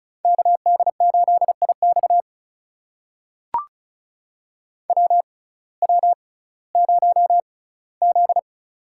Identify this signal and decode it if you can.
Morse code, heard as a keyed tone, about 36 wpm, spaced as for 19 wpm, 700 Hz. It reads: KD8IX WW0Z
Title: USA Call Signs - 6x Character Spacing - 35wpm